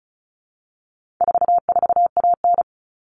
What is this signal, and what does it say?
Morse code, heard as a keyed tone, about 34 wpm, 700 Hz. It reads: 44AN
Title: All Cut Numbers - 4-Digit 35wpm